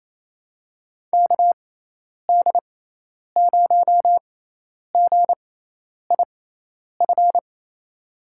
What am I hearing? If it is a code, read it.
Morse code, heard as a keyed tone, about 27 wpm, spaced as for 10 wpm, 700 Hz. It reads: KD0GIF
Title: USA Call Signs - 6x Character Spacing - 28wpm